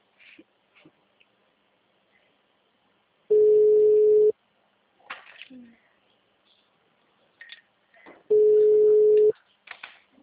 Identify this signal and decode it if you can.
Ringback tone (Europe, 425 Hz), with speech.
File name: out- -2007- - - .97416